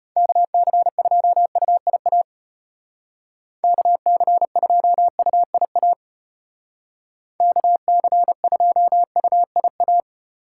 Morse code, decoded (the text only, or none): KC2UIA KC2UIA KC2UIA